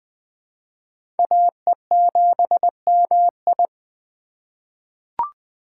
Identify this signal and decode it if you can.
Morse code, heard as a keyed tone, about 20 wpm, 700 Hz. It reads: AE7MI
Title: USA Call Signs - 5 Characters - 2x Character Spacing - 20wpm